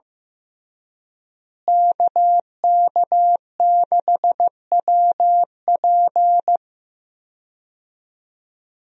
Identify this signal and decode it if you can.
Morse code, heard as a keyed tone, about 15 wpm, 700 Hz. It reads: KK6WP
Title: USA Call Signs - 5 Characters - Repeated 3x - 15wpm